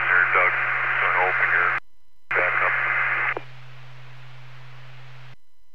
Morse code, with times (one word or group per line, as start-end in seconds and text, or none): none